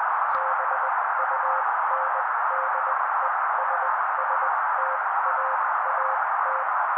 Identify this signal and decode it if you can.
Morse code, heard as a keyed tone, about 20 wpm, spaced as for 16 wpm, 530 Hz. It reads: BUNDESSTAAT
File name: daily / 16wpm noise